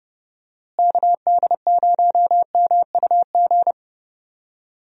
Morse code, heard as a keyed tone, about 30 wpm, 700 Hz. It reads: KD0MUG
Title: USA Call Signs - 6 Characters - Repeated 3x - 30wpm